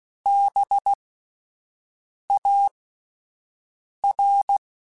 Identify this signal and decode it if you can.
Morse code, heard as a keyed tone, about 16 wpm, spaced as for 5 wpm, 800 Hz. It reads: BAR
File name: Alphabetical / Alphabetical mp3s / Bs @5